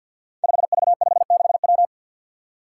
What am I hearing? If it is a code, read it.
Morse code, heard as a keyed tone, about 53 wpm, 700 Hz. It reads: 5V56W
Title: Five Challenging Letter-Number Combinations 50wpm